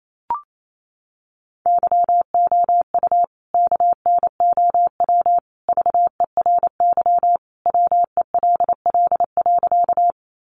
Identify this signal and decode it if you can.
Morse code, heard as a keyed tone, about 28 wpm, 700 Hz. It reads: YOU KNOW VERY WELL.
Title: A Sound Of Thunder 28wpm